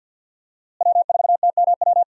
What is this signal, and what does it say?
Morse code, heard as a keyed tone, about 51 wpm, 700 Hz. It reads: W4TMW
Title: USA Call Signs - 5 Characters - Repeated 2x - 50wpm